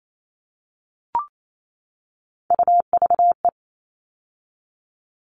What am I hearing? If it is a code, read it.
Morse code, heard as a keyed tone, about 28 wpm, 700 Hz. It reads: UVE
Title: All Cut Numbers - 3-Digit 28wpm